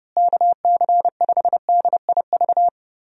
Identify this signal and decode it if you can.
Morse code, heard as a keyed tone, about 30 wpm, 700 Hz. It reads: KC5DIV
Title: USA Call Signs - 6 Characters - Repeated 2x - 30wpm